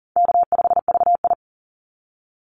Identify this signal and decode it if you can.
Morse code, heard as a keyed tone, about 38 wpm, 700 Hz. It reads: K5VI
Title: Easy Call Signs - 40wpm